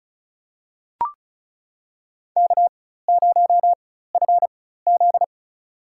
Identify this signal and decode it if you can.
Morse code, heard as a keyed tone, about 34 wpm, spaced as for 16 wpm, 700 Hz. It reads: K0FZ
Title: USA Call Signs - 4 Characters - 4x Character Spacing - 35wpm